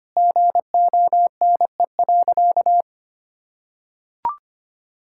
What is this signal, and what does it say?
Morse code, heard as a keyed tone, about 25 wpm, 700 Hz. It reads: GONE.
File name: practice / 25wpm / Sentences with 3-Word Phrases - 25wpm